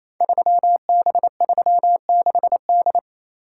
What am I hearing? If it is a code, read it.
Morse code, heard as a keyed tone, about 27 wpm, 700 Hz. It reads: 3B36D
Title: Code Groups - 5 Letter:Numbers - 28wpm